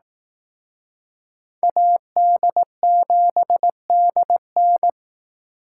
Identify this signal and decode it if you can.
Morse code, heard as a keyed tone, about 18 wpm, 700 Hz. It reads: AD7DN
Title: Speed Racing - USA 2x2 Call Signs 15wpm